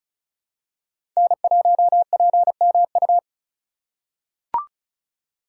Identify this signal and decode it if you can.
Morse code, heard as a keyed tone, about 34 wpm, 700 Hz. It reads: N1PMU
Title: USA Call Signs - 5 Characters - 6x Character Spacing - 35wpm